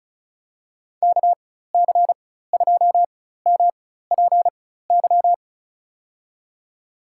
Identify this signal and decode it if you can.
Morse code, heard as a keyed tone, about 36 wpm, spaced as for 16 wpm, 700 Hz. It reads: KC2MPY
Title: USA Call Signs - 6 Characters - 4x Character Spacing - 35wpm